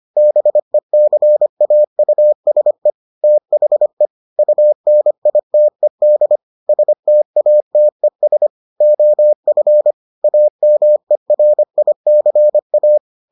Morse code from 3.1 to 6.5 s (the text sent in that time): THE UNITED